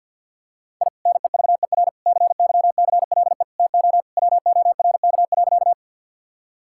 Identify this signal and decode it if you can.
Morse code, heard as a keyed tone, about 50 wpm, 700 Hz. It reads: I NEVER CYCLE TO WORK.
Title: Sentences from Top 1000 Words - Review Part 6 - 50wpm